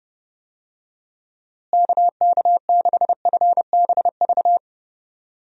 Morse code, heard as a keyed tone, about 30 wpm, 700 Hz. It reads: KK6FBV
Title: USA Call Signs - 6 Characters - Repeated 3x - 30wpm